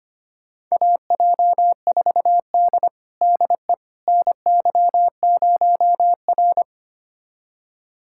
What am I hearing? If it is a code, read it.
Morse code, heard as a keyed tone, about 25 wpm, 700 Hz. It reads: AJ4D DE NY0R
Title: USA Callsign DE Callsign - 4 Characters - 25wpm